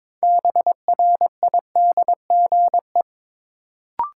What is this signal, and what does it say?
Morse code, heard as a keyed tone, about 21 wpm, 700 Hz. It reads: BRIDGE
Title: Sentences from Top 1200 Words - New Word Drill Alt - 22wpm